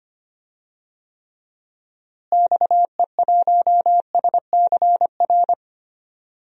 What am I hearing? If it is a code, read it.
Morse code, heard as a keyed tone, about 25 wpm, 700 Hz. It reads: XE1SCR
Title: DXCC-100 Call Signs - 6 Characters - Repeated 3x - 25wpm